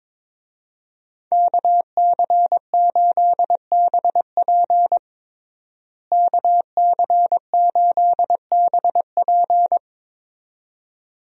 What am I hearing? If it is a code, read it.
Morse code, heard as a keyed tone, about 22 wpm, 700 Hz. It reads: KC8BP KC8BP
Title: USA Call Signs - 5 Characters - Repeated 2x - 22wpm